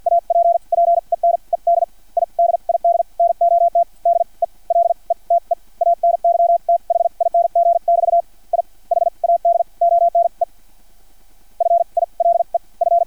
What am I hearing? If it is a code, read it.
Morse code, heard as a keyed tone, about 56 wpm, spaced as for 40 wpm, 690 Hz. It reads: APPENED I DID NOT DELETE ANYTHING <BT> I HAD ONE FILE F
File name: archived / CRSnet